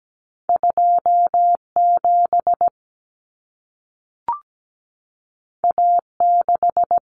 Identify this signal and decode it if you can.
Morse code, heard as a keyed tone, about 17 wpm, 700 Hz. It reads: 27 A6
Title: Common Cut Numbers - 2-Digit 17wpm